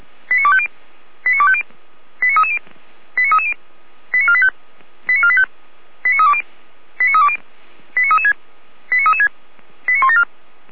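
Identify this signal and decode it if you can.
Five-tone selective call (ZVEI), heard as quick runs of five tones, tones of about 70 ms. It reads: 89250 89250 89330 89330 89485 89485 89320 89320 89336 89336 89174
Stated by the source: Five-tone ZVEI selective call received by SDR, audio.f85360.feuerwehr-orlamuende.org.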